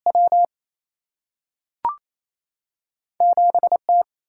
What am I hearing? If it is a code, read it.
Morse code, heard as a keyed tone, about 28 wpm, 700 Hz. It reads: W 7 T
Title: Single-Letter Single-Number 28wpm